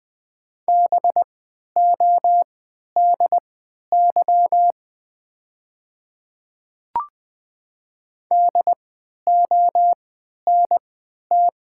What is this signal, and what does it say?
Morse code, heard as a keyed tone, about 20 wpm, spaced as for 11 wpm, 700 Hz. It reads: BODY DONT
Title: 3x Character Spacing - Top 500 Words - 20wpm